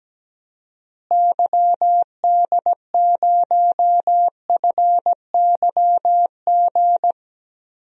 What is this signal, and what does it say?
Morse code, heard as a keyed tone, about 17 wpm, 700 Hz. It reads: YD0FYG
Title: DXCC-100 Call Signs - 6 Characters - Repeated 2x - 17wpm